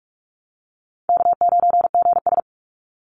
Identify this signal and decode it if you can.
Morse code, heard as a keyed tone, about 44 wpm, 700 Hz. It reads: K9GS
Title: Call Signs Repeated 2x - 45wpm